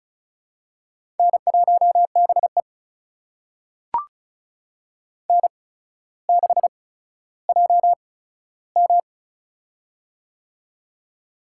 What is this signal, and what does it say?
Morse code, heard as a keyed tone, about 36 wpm, spaced as for 15 wpm, 700 Hz. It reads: N1BE N6JM
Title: USA Call Signs - 4 Characters - 8x Character Spacing - 35wpm